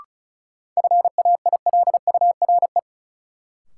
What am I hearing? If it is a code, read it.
Morse code, heard as a keyed tone, about 36 wpm, 700 Hz. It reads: FAILURE
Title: Sentences from Top 900 Words - New Word Drill - 35wpm